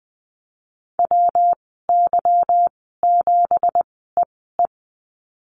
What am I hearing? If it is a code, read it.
Morse code, heard as a keyed tone, about 20 wpm, spaced as for 15 wpm, 700 Hz. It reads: WY7EE
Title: 167 Lesson 27g - Number 7 - Call Signs 20wpm